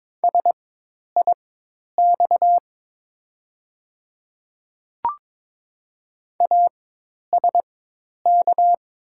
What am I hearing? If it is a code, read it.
Morse code, heard as a keyed tone, about 22 wpm, spaced as for 10 wpm, 700 Hz. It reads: SIX ASK
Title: 4x Character Spacing - Top 500 Words - 22wpm